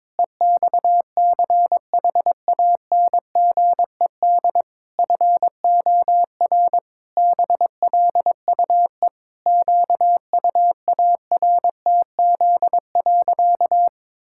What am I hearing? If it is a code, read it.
Morse code, heard as a keyed tone, about 22 wpm, 700 Hz. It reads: EXCHANGED FOR BLUE QUARTZ.
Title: Pangram Challenge v2 - 22wpm